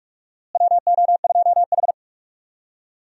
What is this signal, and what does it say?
Morse code, heard as a keyed tone, about 46 wpm, 700 Hz. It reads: WO2H
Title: USA Call Signs - 4 Characters - Repeated 3x - 45wpm